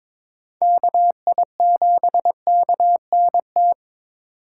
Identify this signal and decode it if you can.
Morse code, heard as a keyed tone, about 22 wpm, 700 Hz. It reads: KI7KNT
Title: USA Call Signs - 6 Characters - 3x Character Spacing - 22wpm